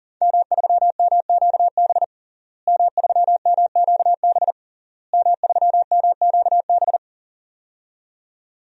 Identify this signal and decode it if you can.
Morse code, heard as a keyed tone, about 40 wpm, 700 Hz. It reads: M3MQB M3MQB M3MQB
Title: DXCC-100 Call Signs - 5 Characters - Repeated 3x - 40wpm